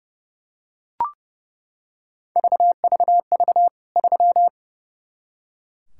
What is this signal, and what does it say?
Morse code, heard as a keyed tone, about 30 wpm, 700 Hz. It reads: VVV 3
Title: K1USN SST QSO Practice 30wpm